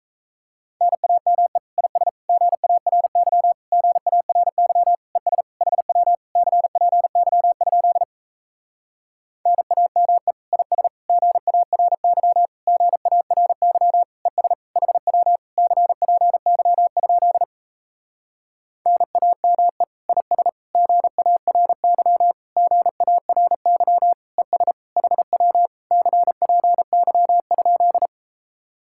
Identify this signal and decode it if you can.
Morse code, heard as a keyed tone, about 38 wpm, 700 Hz. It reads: NAME IS GARY GARY ES HW CPY? NAME IS GARY GARY ES HW CPY? NAME IS GARY GARY ES HW CPY?